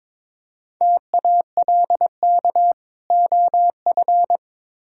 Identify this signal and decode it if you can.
Morse code, heard as a keyed tone, about 22 wpm, 700 Hz. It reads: TALK OF
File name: practice / 22wpm / Build A Phrase - 5-Word Phrases - 22wpm